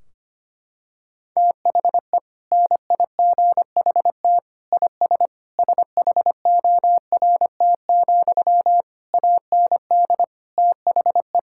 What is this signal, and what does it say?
Morse code, heard as a keyed tone, about 25 wpm, 700 Hz. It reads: THE NIGHT IS SHORT, AND THE
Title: Sentences from Top 1600 Words 25wpm